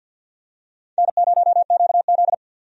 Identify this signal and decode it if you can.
Morse code, heard as a keyed tone, about 51 wpm, 700 Hz. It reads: N0QZ
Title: USA Call Signs - 4 Characters - 50wpm